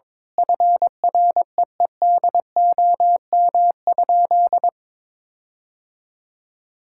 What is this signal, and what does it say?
Morse code, heard as a keyed tone, about 21 wpm, 700 Hz. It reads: FREEDOM?
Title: Sentences from Top 1600 Words 22wpm